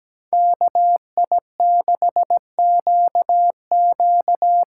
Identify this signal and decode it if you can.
Morse code, heard as a keyed tone, about 17 wpm, 700 Hz. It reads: KI6QQ
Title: USA Call Signs - 8x Character Spacing - 17wpm